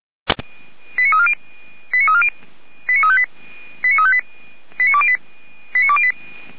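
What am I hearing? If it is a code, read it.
Five-tone selective call (ZVEI), heard as quick runs of five tones, tones of about 70 ms. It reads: 89340 89340 89358 89358 89208
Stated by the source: ZVEI five-tone selective call 89230 received by SDR, from the Feuerwehr Orlamünde fire brigade server.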